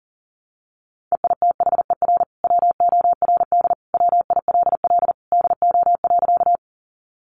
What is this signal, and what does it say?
Morse code, heard as a keyed tone, about 40 wpm, 700 Hz. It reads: EITHER WORD WILL DO.
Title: Sentences from Top 600 Words 40wpm